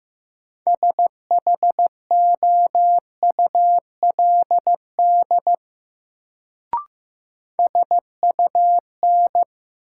Morse code, heard as a keyed tone, about 15 wpm, 700 Hz. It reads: SHOULD SUN
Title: Sentences from Top 300 Words - New Word Drill - 15wpm